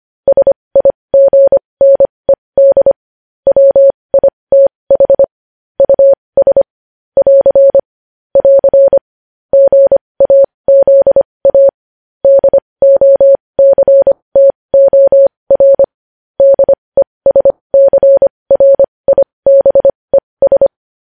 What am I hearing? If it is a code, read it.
Morse code, heard as a keyed tone, about 25 wpm, spaced as for 20 wpm, 550 Hz. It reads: SIGNED WITH US <AR> <AR> GAZA DOCTOR DESCRIBES